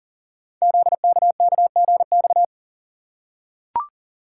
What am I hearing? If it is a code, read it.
Morse code, heard as a keyed tone, about 40 wpm, 700 Hz. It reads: ZKKGX
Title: Code Groups - 5 Letter:Numbers - 40wpm